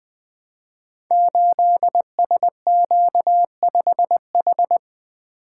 Morse code, heard as a keyed tone, about 20 wpm, 700 Hz. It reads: 8SQ5H
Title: Code Groups - 5 Letter:Numbers - 20wpm